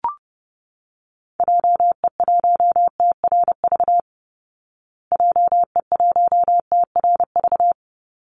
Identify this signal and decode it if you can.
Morse code, heard as a keyed tone, about 30 wpm, 700 Hz. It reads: JE1TRV JE1TRV
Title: Call Signs Repeated 2x - 30wpm